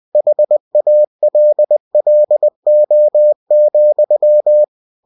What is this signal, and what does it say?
Morse code, heard as a keyed tone, about 20 wpm, 600 Hz. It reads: HALLO,
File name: cw-books / winnie-the-pooh / winnie-20wpm-page-0106